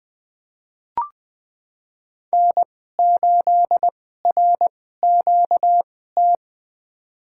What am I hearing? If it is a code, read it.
Morse code, heard as a keyed tone, about 20 wpm, spaced as for 14 wpm, 700 Hz. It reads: N8RQT
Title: USA Call Signs - 2x Character Spacing - 20wpm